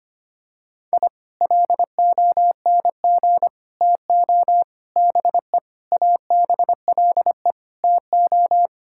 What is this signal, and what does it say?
Morse code, heard as a keyed tone, about 25 wpm, 700 Hz. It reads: I LONG TO BE ABLE TO
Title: Sentences from Top 1000 Words - Review Part 6 - 25wpm